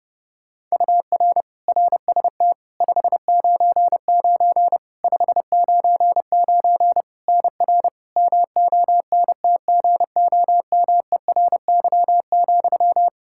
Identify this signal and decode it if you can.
Morse code, heard as a keyed tone, about 30 wpm, 700 Hz. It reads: UR RST 599 599 NR MONTGOMERY,
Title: Practice QSO - Exchange 1 - Beginning - 30wpm